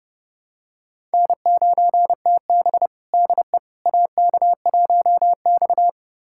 Morse code, heard as a keyed tone, about 30 wpm, 700 Hz. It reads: N9TB DE AK1X
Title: USA Callsign DE Callsign - 4 Characters - 30wpm